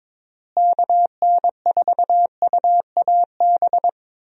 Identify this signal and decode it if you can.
Morse code, heard as a keyed tone, about 22 wpm, 700 Hz. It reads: KN4UAB
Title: USA Call Signs - 6 Characters - Repeated 2x - 22wpm